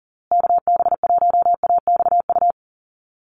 Morse code, heard as a keyed tone, about 40 wpm, 700 Hz. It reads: KB1AXU
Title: USA Call Signs Repeated 3x 40wpm